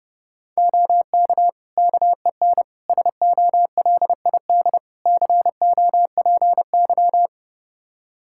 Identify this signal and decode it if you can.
Morse code, heard as a keyed tone, about 32 wpm, 700 Hz. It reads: OK KEN SOLID COPY
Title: Practice QSO - Exchange 3 - More Info - 30wpm